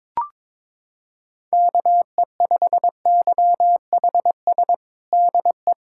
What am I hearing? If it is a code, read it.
Morse code, heard as a keyed tone, about 22 wpm, 700 Hz. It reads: KE5YHS DE
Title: USA Callsign DE Callsign - 6 Characters - 22wpm